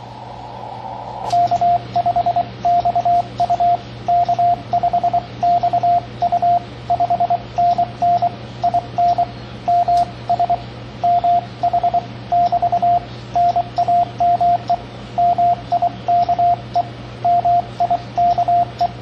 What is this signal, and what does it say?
Morse code, heard as a keyed tone, about 23 wpm, 700 Hz, with radio noise. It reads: K5XU K5XU 5NN IN MS MH <BT> NAME MIKE MIKE
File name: CW Learning Programs / sounds and personalities of cw and transmitters / 16 - K5XU Snags JH0INP on 10 Meters